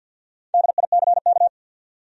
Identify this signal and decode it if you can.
Morse code, heard as a keyed tone, about 53 wpm, 700 Hz. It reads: DICK
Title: USA Common Names - Max 5 Letters - 50wpm